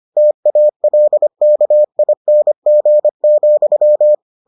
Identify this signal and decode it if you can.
Morse code, heard as a keyed tone, about 25 wpm, 600 Hz. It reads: TALKING,